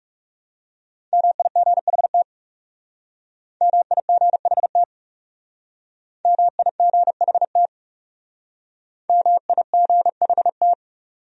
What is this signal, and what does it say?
Morse code, heard as a keyed tone, about 36 wpm, 700 Hz. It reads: MIGHT MIGHT MIGHT MIGHT
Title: Speed-Racing Top 300 Words 30wpm